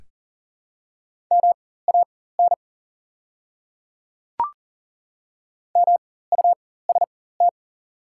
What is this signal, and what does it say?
Morse code, heard as a keyed tone, about 37 wpm, spaced as for 18 wpm, 700 Hz. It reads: MAN MUST
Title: 4x Character Spacing - Top 200 Words - 40wpm